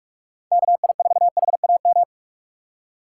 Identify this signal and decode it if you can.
Morse code, heard as a keyed tone, about 46 wpm, 700 Hz. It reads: KI4HAM
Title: USA Call Signs - 6 Characters - 3x Character Spacing - 45wpm